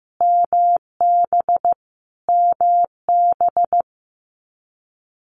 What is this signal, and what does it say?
Morse code, heard as a keyed tone, about 15 wpm, 700 Hz. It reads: MB MB
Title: Multi-letter Phonemes Repeated 3x 15wpm